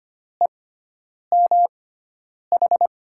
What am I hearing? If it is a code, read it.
Morse code, heard as a keyed tone, about 25 wpm, spaced as for 9 wpm, 700 Hz. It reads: EMH
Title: USA Call Signs - 6 Characters - 6x Character Spacing - 25wpm